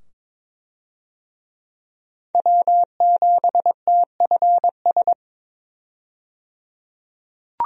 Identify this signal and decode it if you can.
Morse code, heard as a keyed tone, about 22 wpm, 700 Hz. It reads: W7TFS
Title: USA Call Signs - 5 Characters - 22wpm